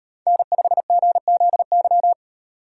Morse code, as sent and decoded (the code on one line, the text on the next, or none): -. ..... --. --.. -.--
N5GZY